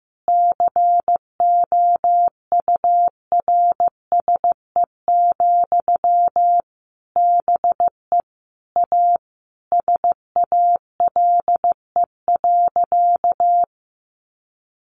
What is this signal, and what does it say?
Morse code, heard as a keyed tone, about 15 wpm, 700 Hz. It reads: COURSE, BE A SALE.
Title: Sentences from Top 1400 Words 15wpm